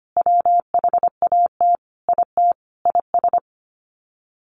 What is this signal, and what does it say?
Morse code, heard as a keyed tone, about 25 wpm, 700 Hz. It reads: WHAT IT IS
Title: Sets of 5 Words Encore 25wpm